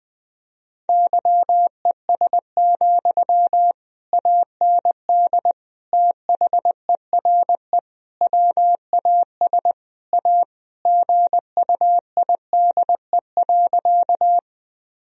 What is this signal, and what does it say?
Morse code, heard as a keyed tone, about 20 wpm, 700 Hz. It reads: YES, AND THERE WAS A GUIDE.
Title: Sentences from Top 1000 Words - Review Part 3 - 20wpm